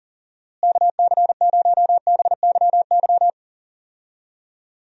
Morse code, heard as a keyed tone, about 40 wpm, 700 Hz. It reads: KC0BYY
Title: USA Call Signs - 6 Characters - Repeated 2x - 40wpm